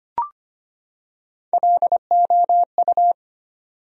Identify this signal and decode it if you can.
Morse code, heard as a keyed tone, about 25 wpm, 700 Hz. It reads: LOU
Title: USA Common Names - Max 3 Letters - 25wpm